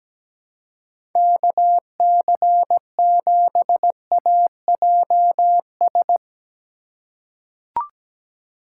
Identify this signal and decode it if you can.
Morse code, heard as a keyed tone, about 17 wpm, 700 Hz. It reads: KC7AJS
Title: USA Call Signs - 6 Characters - 2x Character Spacing - 17wpm